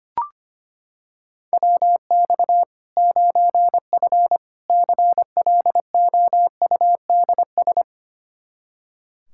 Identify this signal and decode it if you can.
Morse code, heard as a keyed tone, about 25 wpm, 700 Hz. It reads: WX 9F CLOUDS
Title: Practice QSO - Exchange 2 - Info - 25wpm